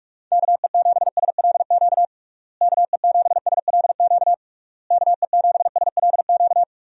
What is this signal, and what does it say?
Morse code, heard as a keyed tone, about 46 wpm, 700 Hz. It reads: KE7SLQ KE7SLQ KE7SLQ
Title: USA Call Signs - 6 Characters - Repeated 3x - 45wpm